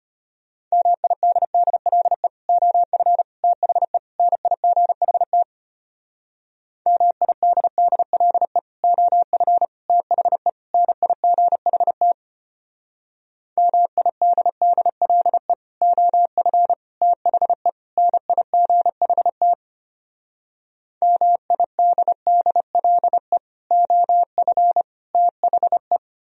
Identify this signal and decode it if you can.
Morse code, heard as a keyed tone, about 33 wpm, 700 Hz. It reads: MIDDLE OF THE NIGHT MIDDLE OF THE NIGHT MIDDLE OF THE NIGHT MIDDLE OF THE